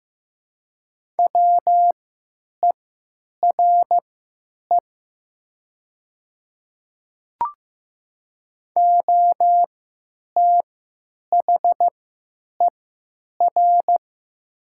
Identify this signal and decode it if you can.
Morse code, heard as a keyed tone, about 15 wpm, spaced as for 9 wpm, 700 Hz. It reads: WERE OTHER
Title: 3x Character Spacing - Top 100 Words - 15wpm